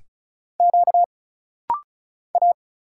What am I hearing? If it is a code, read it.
Morse code, heard as a keyed tone, about 34 wpm, 700 Hz. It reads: Q A
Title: Single Letters - Rapid-Fire 35wpm